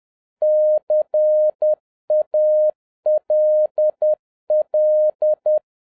Morse code, heard as a keyed tone, about 10 wpm, 610 Hz. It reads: CALL